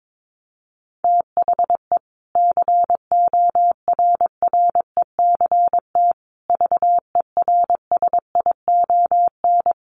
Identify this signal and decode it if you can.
Morse code, heard as a keyed tone, about 22 wpm, 700 Hz. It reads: THE CORRECT VERSION
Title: Sentences from Top 1400 Words 22wpm